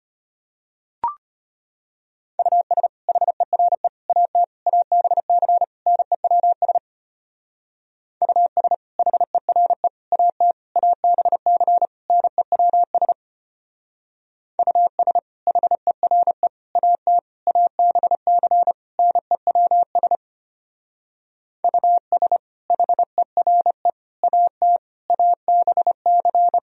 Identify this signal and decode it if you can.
Morse code, heard as a keyed tone, about 32 wpm, 700 Hz. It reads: US HERE AT ABC NEWS US HERE AT ABC NEWS US HERE AT ABC NEWS US HERE AT ABC